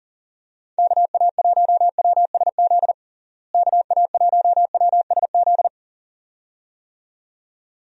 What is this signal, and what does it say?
Morse code, heard as a keyed tone, about 40 wpm, 700 Hz. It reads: KA1WSZ KA1WSZ
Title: USA Call Signs - 6 Characters - Repeated 2x - 40wpm